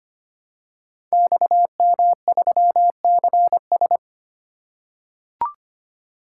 Morse code, heard as a keyed tone, about 25 wpm, 700 Hz. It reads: XM3CS
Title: Code Groups - 5 Letter:Numbers - 25wpm